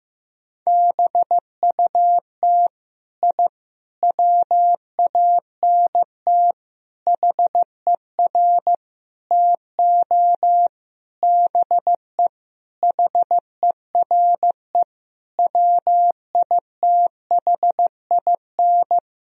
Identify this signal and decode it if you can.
Morse code, heard as a keyed tone, about 15 wpm, 700 Hz. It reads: BUT I WANT HER TO BE HERE WITHIN
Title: Sentences from Top 1000 Words - Review Part 4 - 15wpm